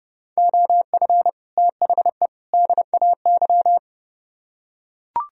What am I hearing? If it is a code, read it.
Morse code, heard as a keyed tone, about 30 wpm, 700 Hz. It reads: OF THE DAY